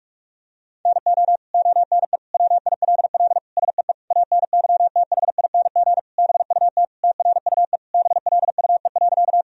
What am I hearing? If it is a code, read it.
Morse code, heard as a keyed tone, about 41 wpm, 700 Hz. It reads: NO ONE WILL SEE ANYTHING BUT TRUE BLUE.
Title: Sentences from Top 1000 Words - Review Part 3 - 45wpm